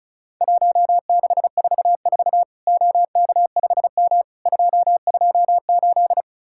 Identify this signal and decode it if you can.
Morse code, heard as a keyed tone, about 36 wpm, 700 Hz. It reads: 1644 OK5M 228
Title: Worked All Europe DX Contest - QTC Exchange - 35wpm